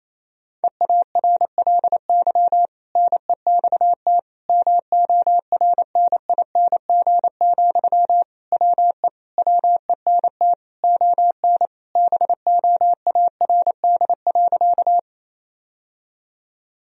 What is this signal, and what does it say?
Morse code, heard as a keyed tone, about 28 wpm, 700 Hz. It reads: EARLY NEXT MORNING, WE WENT ON BOARD.
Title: Sentences from Top 1000 Words - Review Part 4 - 28wpm